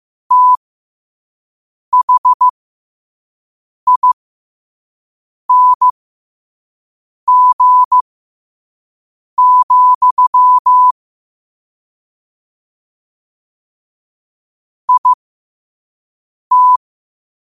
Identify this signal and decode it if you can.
Morse code, heard as a keyed tone, about 15 wpm, spaced as for 5 wpm, 1000 Hz. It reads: THING, IT
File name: qotdfiles / 5wpm